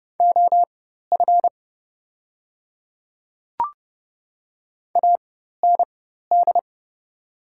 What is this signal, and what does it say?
Morse code, heard as a keyed tone, about 30 wpm, spaced as for 14 wpm, 700 Hz. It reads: OF AND
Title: 4x Character Spacing - Top 100 Words - 30wpm